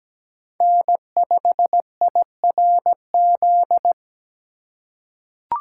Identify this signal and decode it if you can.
Morse code, heard as a keyed tone, about 17 wpm, 700 Hz. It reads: N5IRZ
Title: USA Call Signs - 5 Characters - 3x Character Spacing - 17wpm